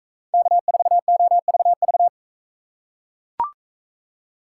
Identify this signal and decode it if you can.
Morse code, heard as a keyed tone, about 42 wpm, 700 Hz. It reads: K4OVV